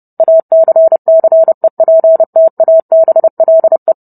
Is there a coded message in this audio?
Yes: Morse code from 0.2 to 3.9 s, heard as a keyed tone, about 30 wpm, 650 Hz. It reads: ACCEPTABLE